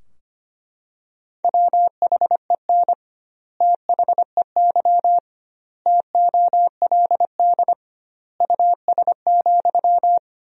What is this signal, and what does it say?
Morse code, heard as a keyed tone, about 25 wpm, 700 Hz. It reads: WHEN THEY TOLD US,
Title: 2x Word-Spacing - Sentences from Top 400 Words - 25wpm